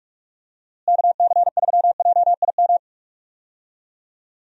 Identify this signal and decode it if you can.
Morse code, heard as a keyed tone, about 42 wpm, 700 Hz. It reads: KC3JIM
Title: USA Call Signs - 6 Characters - Repeated 3x - 45wpm